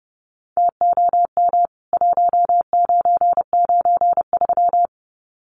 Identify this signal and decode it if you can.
Morse code, heard as a keyed tone, about 30 wpm, 700 Hz. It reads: TOM 1993
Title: CWT Exchange 30wpm